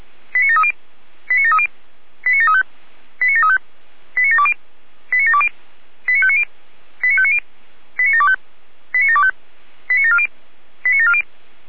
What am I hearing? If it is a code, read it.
Five-tone selective call (ZVEI), heard as quick runs of five tones, tones of about 70 ms. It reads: 89730 89730 89735 89735 89720 89720 89590 89590 89725 89725 89740 89740
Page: SDR/ZVEI